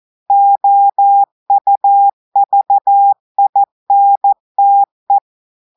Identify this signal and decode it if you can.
Morse code, heard as a keyed tone, about 14 wpm, 800 Hz. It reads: OUVINTE
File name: audios / portugais / portugais texte en clair 2 vitesse 14 mots minutes